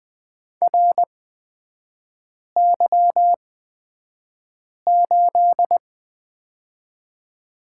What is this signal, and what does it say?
Morse code, heard as a keyed tone, about 20 wpm, spaced as for 5 wpm, 700 Hz. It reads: RY8
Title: 243 Lesson 39e - Letter z - Three Characters 20wpm